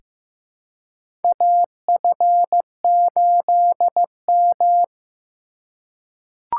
Morse code, heard as a keyed tone, about 15 wpm, 700 Hz. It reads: AF8M
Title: USA Call Signs - 4 Characters - 2x Character Spacing - 15wpm